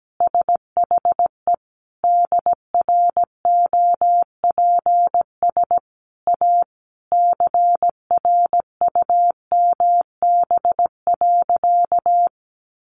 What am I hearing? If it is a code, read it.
Morse code, heard as a keyed tone, about 17 wpm, 700 Hz. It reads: SHE DROPS A CRUMB.
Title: Additional Easy Phrases 17wpm